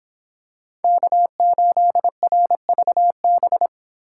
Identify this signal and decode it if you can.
Morse code, heard as a keyed tone, about 26 wpm, 700 Hz. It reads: K8RVB